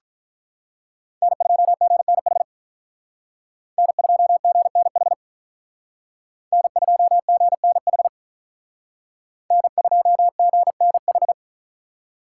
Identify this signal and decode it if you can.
Morse code, heard as a keyed tone, about 45 wpm, 700 Hz. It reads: N2GNH N2GNH N2GNH N2GNH